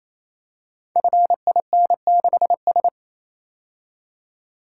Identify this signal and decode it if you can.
Morse code, heard as a keyed tone, about 27 wpm, 700 Hz. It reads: FIN6S
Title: Code Groups - 5 Letter:Numbers - 28wpm